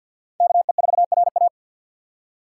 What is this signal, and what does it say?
Morse code, heard as a keyed tone, about 53 wpm, 700 Hz. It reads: KE4RA
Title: USA Call Signs - 5 Characters - 4x Character Spacing - 50wpm